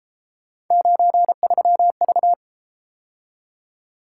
Morse code, heard as a keyed tone, about 33 wpm, 700 Hz. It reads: 93V